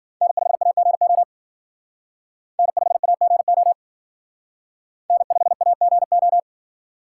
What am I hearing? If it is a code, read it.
Morse code, heard as a keyed tone, about 51 wpm, 700 Hz. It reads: N5AGO N5AGO N5AGO